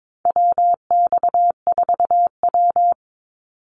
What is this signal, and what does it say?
Morse code, heard as a keyed tone, about 22 wpm, 690 Hz. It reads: WX4W
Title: Call Signs Repeated 2x - 22wpm